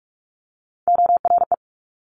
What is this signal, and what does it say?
Morse code, heard as a keyed tone, about 46 wpm, 700 Hz. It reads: ORE
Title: Multi-letter Phonemes Quick Fire 45wpm